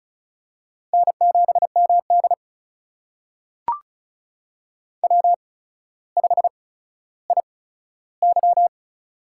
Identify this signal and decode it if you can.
Morse code, heard as a keyed tone, about 36 wpm, spaced as for 15 wpm, 700 Hz. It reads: N7MD W5IY
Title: USA Call Signs - 4 Characters - 8x Character Spacing - 35wpm